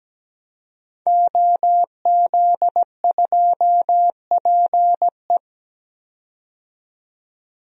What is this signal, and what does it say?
Morse code, heard as a keyed tone, about 17 wpm, 700 Hz. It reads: OZ2PE
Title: DXCC-100 Call Signs - 5 Characters - 17wpm